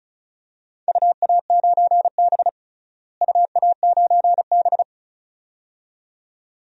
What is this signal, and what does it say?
Morse code, heard as a keyed tone, about 36 wpm, 700 Hz. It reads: UA9B UA9B
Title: DXCC-100 Call Signs - 4 Characters - Repeated 2x - 35wpm